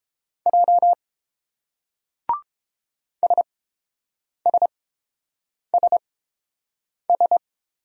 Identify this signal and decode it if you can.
Morse code, heard as a keyed tone, about 29 wpm, spaced as for 7 wpm, 700 Hz. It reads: J SSSS